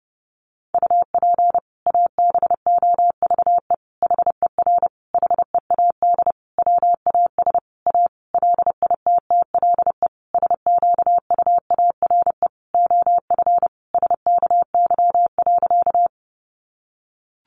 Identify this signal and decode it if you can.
Morse code, heard as a keyed tone, about 30 wpm, 700 Hz. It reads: UP ABOVE HER HEAD WAS A LITTLE SQUARE OF SKY.
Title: Sentences from Top 1200 Words 30wpm